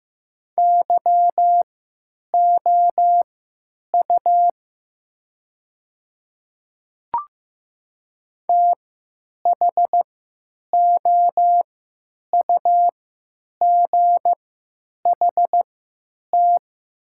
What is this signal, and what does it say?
Morse code, heard as a keyed tone, about 15 wpm, spaced as for 9 wpm, 700 Hz. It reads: YOU THOUGHT
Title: 3x Character Spacing - Top 300 Words - 15wpm